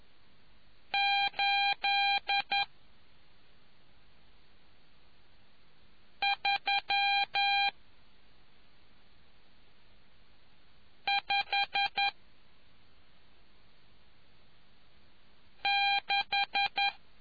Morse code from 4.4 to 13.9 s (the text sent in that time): 3 5